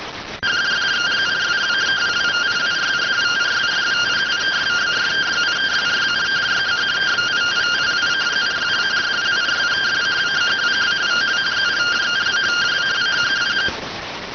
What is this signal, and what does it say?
Two-tone teleprinter signal, about 170 Hz shift.